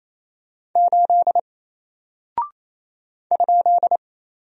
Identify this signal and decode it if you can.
Morse code, heard as a keyed tone, about 28 wpm, 700 Hz. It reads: 8 ?
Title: Single Character (Letter:Number:Punctuation:Prosign) Alt 28wpm